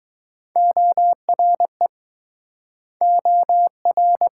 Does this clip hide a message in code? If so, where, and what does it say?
Yes: Morse code from 0.6 to 4.3 s, heard as a keyed tone, about 23 wpm, 690 Hz. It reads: ORE OR